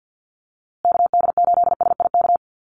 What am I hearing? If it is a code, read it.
Morse code, heard as a keyed tone, about 50 wpm, 700 Hz. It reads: KD8SIK
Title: USA Call Signs Repeated 3x 50wpm